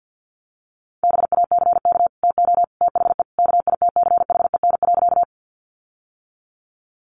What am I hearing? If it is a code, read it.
Morse code, heard as a keyed tone, about 50 wpm, 700 Hz. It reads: BACK TO THE KITCHEN.
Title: Sentences from Top 1400 Words 50wpm